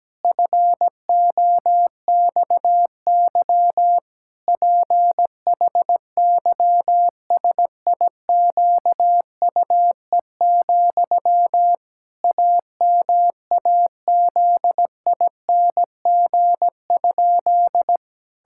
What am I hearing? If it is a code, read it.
Morse code, heard as a keyed tone, about 17 wpm, 700 Hz. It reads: FOXY PHYSIQUE, AMAZING?
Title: Pangram Challenge v2 - 17wpm